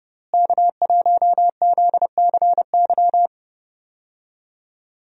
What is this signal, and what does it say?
Morse code, heard as a keyed tone, about 30 wpm, 700 Hz. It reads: K1ZCY
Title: USA Call Signs - 5 Characters - Repeated 3x - 30wpm